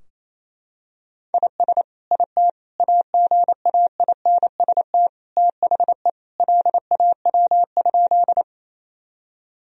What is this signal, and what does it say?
Morse code, heard as a keyed tone, about 28 wpm, 700 Hz. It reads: IS IT AGAINST THE LAW?
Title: Sentences from Top 1000 Words - Review Part 3 - 28wpm